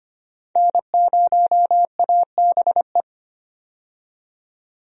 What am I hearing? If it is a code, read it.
Morse code, heard as a keyed tone, about 25 wpm, 700 Hz. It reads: N0ABE
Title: USA Call Signs - 5 Characters - 25wpm